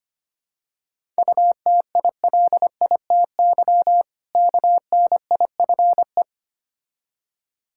Morse code, heard as a keyed tone, about 25 wpm, 700 Hz. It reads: UTILITY KNIFE
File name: practice / 25wpm / Difficult Words with Hints - 25wpm